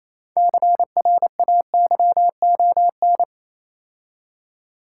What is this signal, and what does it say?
Morse code, heard as a keyed tone, about 27 wpm, 690 Hz. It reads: CRAYON
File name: Easy Words - Speed-Racing with Hints - 22wpm